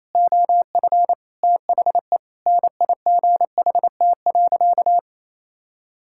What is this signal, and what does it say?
Morse code, heard as a keyed tone, about 28 wpm, 700 Hz. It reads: OF THE NIGHT.
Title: Sentences from Top 1000 Words - Review Part 6 - 28wpm